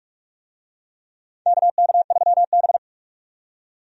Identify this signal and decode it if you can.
Morse code, heard as a keyed tone, about 45 wpm, 700 Hz. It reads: KK3B
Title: USA Call Signs - 4 Characters - 45wpm